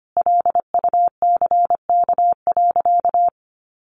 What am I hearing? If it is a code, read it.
Morse code, heard as a keyed tone, about 25 wpm, 700 Hz. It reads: LUCK.
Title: Sentences from Top 1500 Words 25wpm